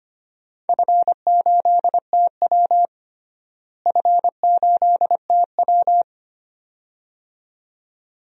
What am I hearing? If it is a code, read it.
Morse code, heard as a keyed tone, about 25 wpm, 700 Hz. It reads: F8TW F8TW
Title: DXCC-100 Call Signs - 4 Characters - Repeated 2x - 25wpm